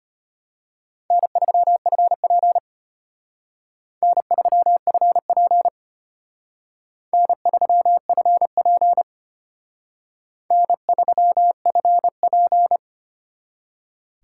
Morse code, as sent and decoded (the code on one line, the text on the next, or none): -. ...-- ..-. .--. / -. ...-- ..-. .--. / -. ...-- ..-. .--. / -. ...-- ..-. .--.
N3FP N3FP N3FP N3FP